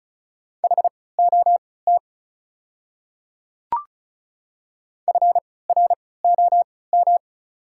Morse code, heard as a keyed tone, about 36 wpm, spaced as for 20 wpm, 690 Hz. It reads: HOT FROM
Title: 3x Character Spacing - Top 100 Words - 35wpm